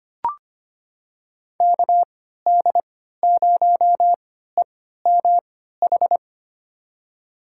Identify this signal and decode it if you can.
Morse code, heard as a keyed tone, about 25 wpm, spaced as for 14 wpm, 700 Hz. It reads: KD0EMH
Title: USA Call Signs - 6 Characters - 3x Character Spacing - 25wpm